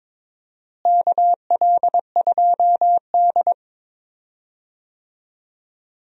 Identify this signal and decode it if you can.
Morse code, heard as a keyed tone, about 22 wpm, 700 Hz. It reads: KL2D
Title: USA Call Signs - 4 Characters - Repeated 2x - 22wpm